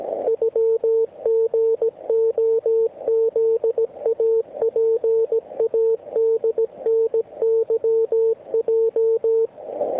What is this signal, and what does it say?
Morse code (Russian alphabet), heard as a keyed tone, about 17 wpm, 450 Hz, with radio noise. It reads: ЮГОЗАПАДНЫЙ
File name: ARCHIV / zvuk3 / WX RMP 3748